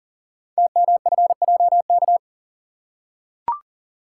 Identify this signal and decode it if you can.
Morse code, heard as a keyed tone, about 39 wpm, 700 Hz. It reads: TMFJK
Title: Code Groups - 5 Letter:Numbers - 40wpm